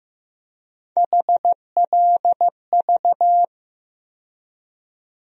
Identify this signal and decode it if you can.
Morse code, heard as a keyed tone, about 15 wpm, 700 Hz. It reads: HLV
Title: USA Call Sign Postfixes - 3 Letters 15wpm